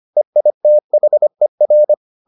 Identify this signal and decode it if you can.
Morse code, heard as a keyed tone, about 25 wpm, 600 Hz. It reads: EITHER